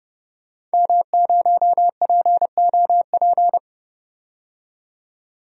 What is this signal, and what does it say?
Morse code, heard as a keyed tone, about 31 wpm, 700 Hz. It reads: M0POP
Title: DXCC-100 Call Signs - 5 Characters - 30wpm